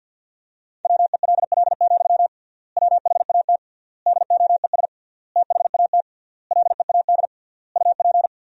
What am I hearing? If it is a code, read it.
Morse code, heard as a keyed tone, about 51 wpm, 700 Hz. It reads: WELL, WHAT DOES THAT LEAD UP
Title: 3x Word-Spacing - Sentences from Top 500 Words - 50wpm